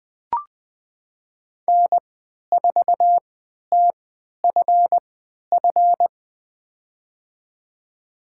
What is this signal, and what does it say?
Morse code, heard as a keyed tone, about 20 wpm, spaced as for 11 wpm, 700 Hz. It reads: N4TFF
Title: USA Call Signs - 3x Character Spacing - 20wpm